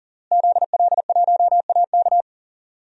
Morse code, heard as a keyed tone, about 40 wpm, 700 Hz. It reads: ZL1AK
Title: DXCC-100 Call Signs - 5 Characters - Repeated 3x - 40wpm